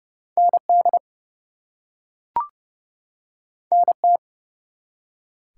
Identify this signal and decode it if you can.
Morse code, heard as a keyed tone, about 30 wpm, 700 Hz. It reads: ND NT
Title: US States and Canadian Provinces 30wpm